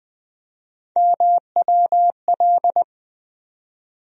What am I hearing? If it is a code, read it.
Morse code, heard as a keyed tone, about 20 wpm, 700 Hz. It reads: MWL
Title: USA Call Sign Postfixes - 3 Letters 20wpm